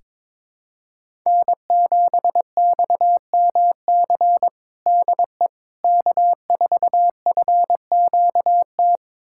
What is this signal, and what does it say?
Morse code, heard as a keyed tone, about 22 wpm, 700 Hz. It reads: N7XMC DE K4FQT
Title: USA Callsign DE Callsign 22wpm